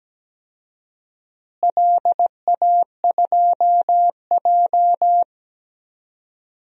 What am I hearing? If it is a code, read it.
Morse code, heard as a keyed tone, about 17 wpm, 700 Hz. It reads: LA2J
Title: DXCC-100 Call Signs - 4 Characters - Repeated 2x - 17wpm